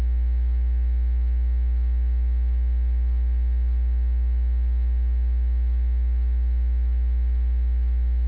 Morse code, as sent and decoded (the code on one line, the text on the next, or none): none